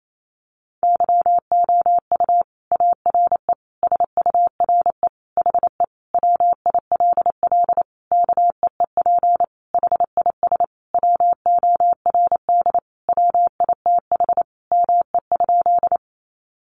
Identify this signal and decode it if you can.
Morse code, heard as a keyed tone, about 28 wpm, 700 Hz. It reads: YOU ARE SURE HE WILL KEEP HIS WORD WITH ME?
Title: Sentences from Top 300 Words 28wpm